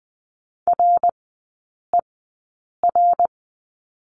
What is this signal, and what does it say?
Morse code, heard as a keyed tone, about 20 wpm, spaced as for 8 wpm, 700 Hz. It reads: RER
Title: 053 Lesson 10e - Letter r - Three Characters 20wpm